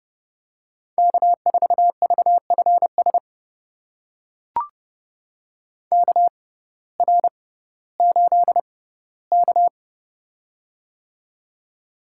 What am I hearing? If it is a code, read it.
Morse code, heard as a keyed tone, about 30 wpm, spaced as for 17 wpm, 700 Hz. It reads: K4VFS KR8K
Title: USA Call Signs - 6x Character Spacing - 30wpm